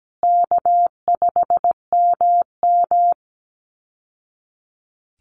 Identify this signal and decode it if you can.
Morse code, heard as a keyed tone, about 17 wpm, 700 Hz. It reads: K5MM
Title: Call Signs Repeated 2x - 17wpm